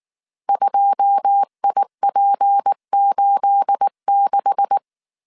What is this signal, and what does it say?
Morse code, heard as a keyed tone, about 19 wpm, 800 Hz. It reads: 2IP86
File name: CW / DET / 1 9 1500C S A 18wpm 2459